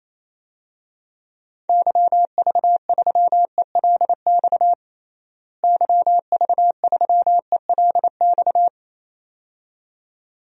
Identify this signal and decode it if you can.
Morse code, heard as a keyed tone, about 28 wpm, 700 Hz. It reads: YV3ELX YV3ELX
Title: DXCC-100 Call Signs - 6 Characters - Repeated 2x - 28wpm